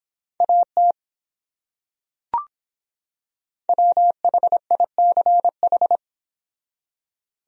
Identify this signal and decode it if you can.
Morse code, heard as a keyed tone, about 25 wpm, 700 Hz. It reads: AT WHICH